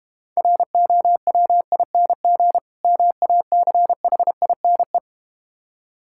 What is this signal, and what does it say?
Morse code, heard as a keyed tone, about 33 wpm, 700 Hz. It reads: ROWING MACHINE